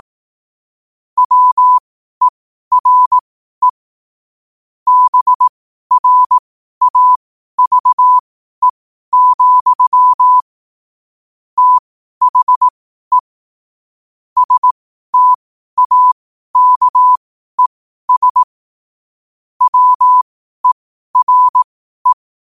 Morse code, heard as a keyed tone, about 18 wpm, spaced as for 12 wpm, 1000 Hz. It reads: WERE BRAVE, THE STAKES WERE
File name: qotdfiles / 12wpm